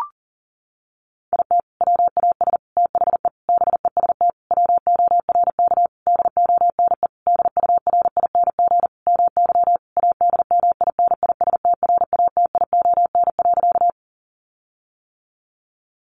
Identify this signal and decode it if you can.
Morse code, heard as a keyed tone, about 40 wpm, 700 Hz. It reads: IT WAS THE BEST WORK DONE DURING MY ADMINISTRATION.
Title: Sentences from Top 1300 Words 40wpm